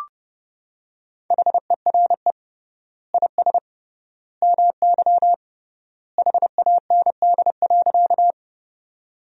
Morse code, heard as a keyed tone, about 31 wpm, 700 Hz. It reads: HERE IS MY HAND.
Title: 3x Word-Spacing - Sentences from Top 200 Words - 30wpm